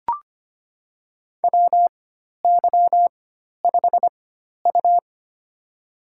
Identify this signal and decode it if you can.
Morse code, heard as a keyed tone, about 25 wpm, spaced as for 12 wpm, 700 Hz. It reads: WY5U
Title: USA Call Signs - 4 Characters - 4x Character Spacing - 25wpm